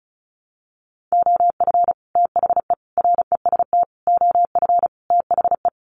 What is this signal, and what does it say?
Morse code, heard as a keyed tone, about 34 wpm, 700 Hz. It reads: OF THE REST OF THE
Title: Sets of 5 Words 35wpm